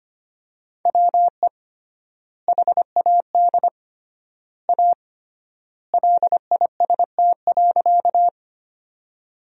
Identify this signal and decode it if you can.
Morse code, heard as a keyed tone, about 25 wpm, 700 Hz. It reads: WE HAD A LIST.
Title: 3x Word-Spacing - Sentences from Top 400 Words - 25wpm